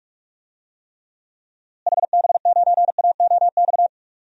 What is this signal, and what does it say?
Morse code, heard as a keyed tone, about 43 wpm, 700 Hz. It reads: HB9AOX
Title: DXCC-100 Call Signs - 6 Characters - Repeated 2x - 45wpm